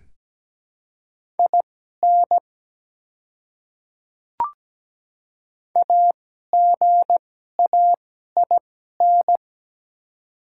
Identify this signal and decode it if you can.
Morse code, heard as a keyed tone, about 17 wpm, spaced as for 12 wpm, 700 Hz. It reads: IN AGAIN
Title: 2x Character Spacing - Top 200 Words - 17wpm